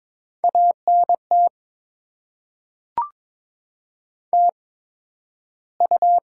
Morse code, heard as a keyed tone, about 22 wpm, spaced as for 12 wpm, 700 Hz. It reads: ANT TU
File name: Top 20 QSO Elements - 8x Character Spacing - 22wpm